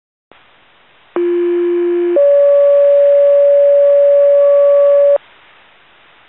Two-tone page, heard as a one-second tone, then a three-second tone, about 350 then 570 Hz. It is heard from 1.2 to 5.2 s.